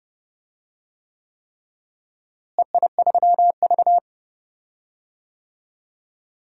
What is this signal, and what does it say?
Morse code, heard as a keyed tone, about 30 wpm, 700 Hz. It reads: EI3V
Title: DXCC-100 Call Signs - 4 Characters - Repeated 3x - 30wpm